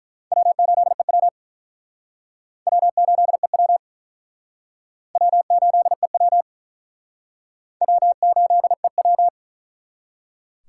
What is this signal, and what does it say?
Morse code, heard as a keyed tone, about 48 wpm, 700 Hz. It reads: W8EW W8EW W8EW W8EW